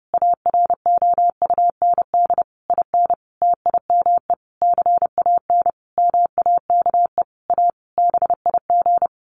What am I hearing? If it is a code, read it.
Morse code, heard as a keyed tone, about 30 wpm, 700 Hz. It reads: AROUND IN TIME CAN MAKE A BIG
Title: A Sound Of Thunder 30wpm